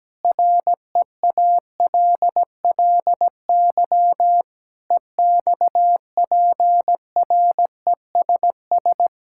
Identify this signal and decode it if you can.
Morse code, heard as a keyed tone, about 17 wpm, 700 Hz. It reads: REALLY EXPRESS
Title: Sentences from Top 1600 Words 17wpm